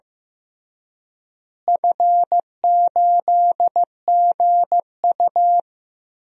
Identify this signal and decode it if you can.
Morse code, heard as a keyed tone, about 15 wpm, 700 Hz. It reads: F8GU
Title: DXCC-100 Call Signs - 4 Characters - Repeated 3x - 15wpm